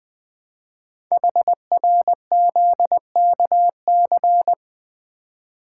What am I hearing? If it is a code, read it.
Morse code, heard as a keyed tone, about 20 wpm, 700 Hz. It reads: HRZKC
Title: Code Groups - 5 Letters - 20wpm